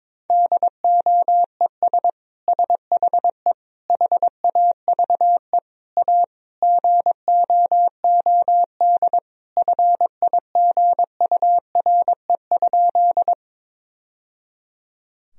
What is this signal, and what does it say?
Morse code, heard as a keyed tone, about 22 wpm, 700 Hz. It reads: DOES SHE HAVE A GOOD FIGURE?
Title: Sentences from Top 1000 Words - Review Part 2 - 22wpm